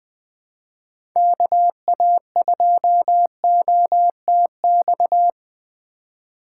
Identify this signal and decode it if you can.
Morse code, heard as a keyed tone, about 20 wpm, 700 Hz. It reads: KA2OTX
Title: USA Call Signs - 2x Character Spacing - 20wpm